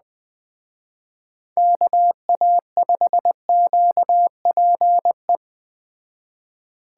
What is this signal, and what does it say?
Morse code, heard as a keyed tone, about 20 wpm, 700 Hz. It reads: KA5QPE
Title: Speed Racing - USA 2x3 Call Signs 17wpm